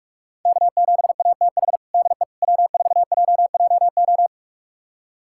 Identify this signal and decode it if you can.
Morse code, heard as a keyed tone, about 46 wpm, 700 Hz. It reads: K7ATH DE W4JJO
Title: USA Callsign DE Callsign - 5 Characters - 45wpm